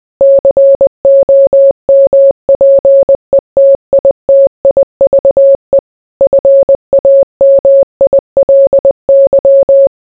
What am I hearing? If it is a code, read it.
Morse code, heard as a keyed tone, about 20 wpm, 550 Hz. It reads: COMPETITIVE FAMILY